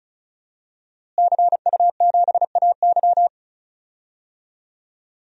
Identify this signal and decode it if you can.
Morse code, heard as a keyed tone, about 36 wpm, 700 Hz. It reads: CU7AY
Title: DXCC-100 Call Signs - 5 Characters - Repeated 3x - 35wpm